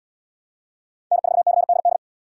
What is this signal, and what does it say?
Morse code, heard as a keyed tone, about 68 wpm, 700 Hz. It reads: N4CDD